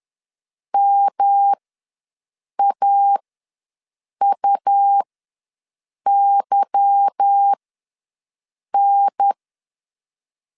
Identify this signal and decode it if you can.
Morse code, heard as a keyed tone, about 10 wpm, spaced as for 6 wpm, 800 Hz. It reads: MAUYN